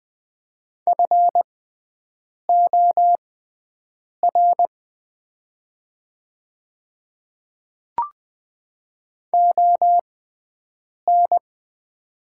Morse code, heard as a keyed tone, about 20 wpm, spaced as for 7 wpm, 700 Hz. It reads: FOR ON
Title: 6x Character Spacing - Top 100 Words - 20wpm